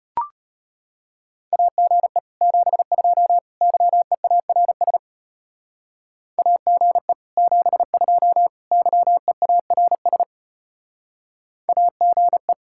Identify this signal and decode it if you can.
Morse code, heard as a keyed tone, about 32 wpm, 700 Hz. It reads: AGE 72 YEARS AGE 72 YEARS AGE